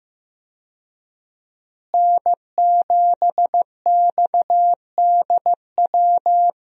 Morse code, heard as a keyed tone, about 15 wpm, 700 Hz. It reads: N7XDW
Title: USA Call Signs - 5 Characters - Repeated 3x - 15wpm